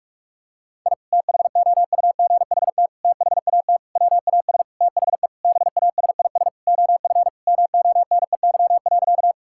Morse code, heard as a keyed tone, about 46 wpm, 700 Hz. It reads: I THOUGHT THAT WAS THE BASIS OF MONEY.
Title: Sentences from Top 1000 Words - Review Part 5 - 45wpm